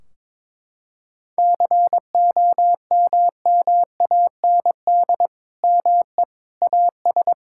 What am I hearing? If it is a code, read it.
Morse code, heard as a keyed tone, about 22 wpm, 700 Hz. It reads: COMMAND ME AS
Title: Sentences from Top 1600 Words 22wpm